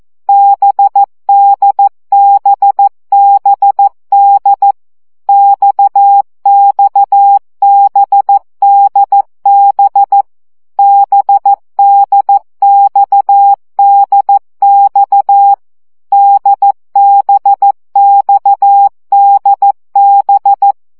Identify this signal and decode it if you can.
Morse code, heard as a keyed tone, about 14 wpm, 800 Hz. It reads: BDBBD XXBDB BDXDX DBXDB
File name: CW gyakorlas d b x 12 baud